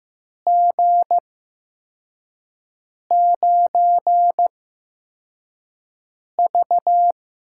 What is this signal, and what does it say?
Morse code, heard as a keyed tone, about 15 wpm, spaced as for 4 wpm, 700 Hz. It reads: G9V
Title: USA Call Signs - 4 Characters - 8x Character Spacing - 15wpm